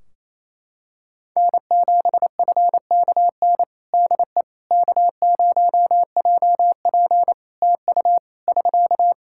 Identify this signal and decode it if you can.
Morse code, heard as a keyed tone, about 28 wpm, 700 Hz. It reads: N7FKN DE K0JP TU <SK>
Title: Practice QSO - Exchange 4 - Ending - 28wpm